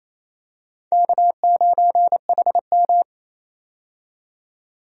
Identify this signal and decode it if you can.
Morse code, heard as a keyed tone, about 28 wpm, 700 Hz. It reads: K9HM
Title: USA Call Signs - 4 Characters - 28wpm